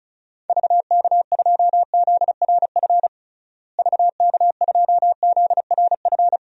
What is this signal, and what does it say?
Morse code, heard as a keyed tone, about 36 wpm, 700 Hz. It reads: VK2ZRF VK2ZRF
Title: DXCC-100 Call Signs - 6 Characters - Repeated 2x - 35wpm